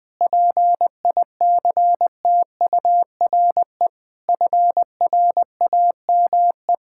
Morse code, heard as a keyed tone, about 21 wpm, 700 Hz. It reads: PICTURE FRAME
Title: Sentences from Top 1000 Words - Review Part 6 - 20wpm